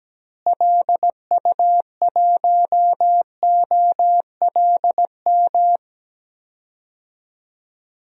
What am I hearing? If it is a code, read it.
Morse code, heard as a keyed tone, about 17 wpm, 700 Hz. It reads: LU1OLM
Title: DXCC-100 Call Signs - 6 Characters - Repeated 3x - 17wpm